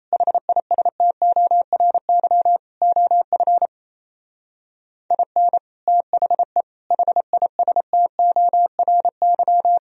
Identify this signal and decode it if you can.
Morse code, heard as a keyed tone, about 34 wpm, 700 Hz. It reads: HISTORY OF IN THE HISTORY
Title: Speed-Racing - Sets of 4 Words 28wpm